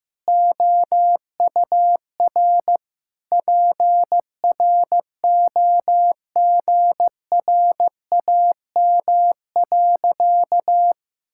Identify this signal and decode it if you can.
Morse code, heard as a keyed tone, about 15 wpm, 700 Hz. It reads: OUR PROGRAM.
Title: Sentences from Top 1000 Words - Review Part 5 - 15wpm